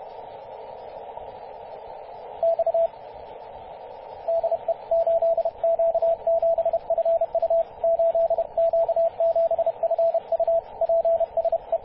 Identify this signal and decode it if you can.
Morse code, heard as a keyed tone, about 31 wpm, 670 Hz, with radio noise. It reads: X DE 8Q7FU 8Q7FU PSE
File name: 8q7fu-24-896mhz-cw